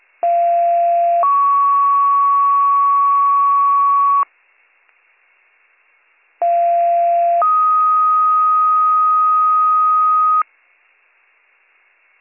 Two-tone page, heard as a one-second tone, then a three-second tone, about 690 then 1120 Hz, about 690 then 1250 Hz. It is heard from 0.2 to 10.4 s.